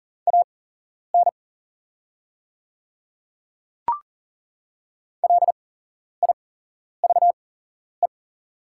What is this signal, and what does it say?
Morse code, heard as a keyed tone, about 40 wpm, spaced as for 11 wpm, 690 Hz. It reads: AN LIVE
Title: 8x Character Spacing - Top 300 Words - 40wpm